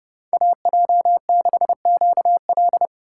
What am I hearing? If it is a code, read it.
Morse code, heard as a keyed tone, about 31 wpm, 700 Hz. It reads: AJ6QL
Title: K1USN SST QSO Practice - 30wpm